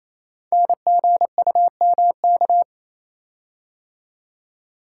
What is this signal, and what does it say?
Morse code, heard as a keyed tone, about 27 wpm, 700 Hz. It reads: NGUMK
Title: Code Groups - 5 Letters - 28wpm